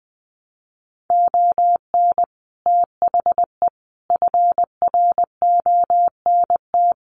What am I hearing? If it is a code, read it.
Morse code, heard as a keyed tone, about 20 wpm, 700 Hz. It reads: ON THE FRONT
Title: Sets of 5 Words Encore 20wpm